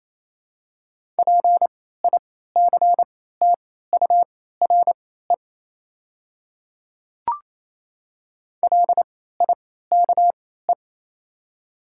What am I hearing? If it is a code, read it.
Morse code, heard as a keyed tone, about 28 wpm, spaced as for 16 wpm, 700 Hz. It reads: PICTURE LIKE
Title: 3x Character Spacing - Top 200 Words - 28wpm